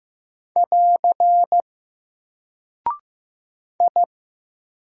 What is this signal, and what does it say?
Morse code, heard as a keyed tone, about 15 wpm, 700 Hz. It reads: <AR> I
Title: Single Character (Letter:Number:Punctuation:Prosign) 15wpm